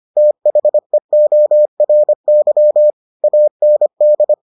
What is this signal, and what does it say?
Morse code, heard as a keyed tone, about 25 wpm, 600 Hz. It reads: THEORY AND